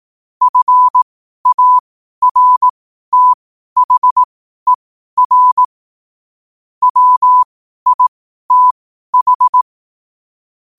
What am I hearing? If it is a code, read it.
Morse code, heard as a keyed tone, about 18 wpm, spaced as for 12 wpm, 1000 Hz. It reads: FARTHER WITH
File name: qotdfiles / 12wpm